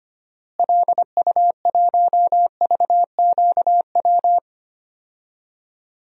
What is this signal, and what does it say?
Morse code, heard as a keyed tone, about 25 wpm, 700 Hz. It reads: LU1VQW
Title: DXCC-100 Call Signs - 6 Characters - Repeated 2x - 25wpm